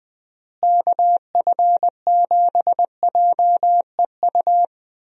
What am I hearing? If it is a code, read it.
Morse code, heard as a keyed tone, about 20 wpm, 700 Hz. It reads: KF7JEU
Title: USA Call Signs - 6 Characters - Repeated 3x - 20wpm